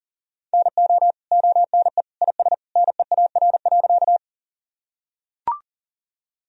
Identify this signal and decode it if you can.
Morse code, heard as a keyed tone, about 40 wpm, 700 Hz. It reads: NO ONE IS NEAR.
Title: Sentences from Top 1000 Words - Review Part 1 - 40wpm